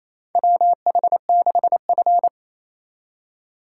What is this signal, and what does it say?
Morse code, heard as a keyed tone, about 28 wpm, 700 Hz. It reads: WH6F
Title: USA Call Signs - 4 Characters - 4x Character Spacing - 28wpm